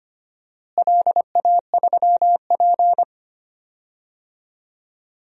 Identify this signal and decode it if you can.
Morse code, heard as a keyed tone, about 25 wpm, 700 Hz. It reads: LA3P
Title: DXCC-100 Call Signs - 4 Characters - Repeated 3x - 25wpm